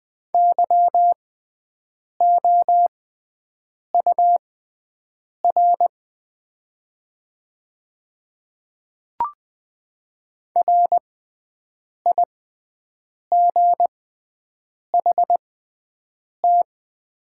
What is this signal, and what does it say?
Morse code, heard as a keyed tone, about 20 wpm, spaced as for 7 wpm, 700 Hz. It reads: YOUR RIGHT
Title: 6x Character Spacing - Top 300 Words - 20wpm